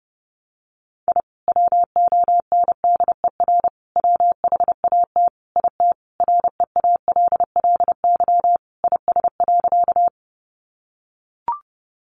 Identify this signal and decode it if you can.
Morse code, heard as a keyed tone, about 28 wpm, 700 Hz. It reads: I WONDER WHAT IT REALLY IS.
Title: Sentences from Top 600 Words 30wpm